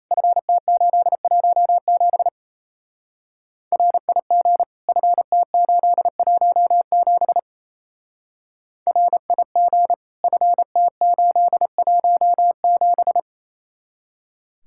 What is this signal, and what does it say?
Morse code, heard as a keyed tone, about 35 wpm, 700 Hz. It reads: FT817 RIG FT817 RIG FT817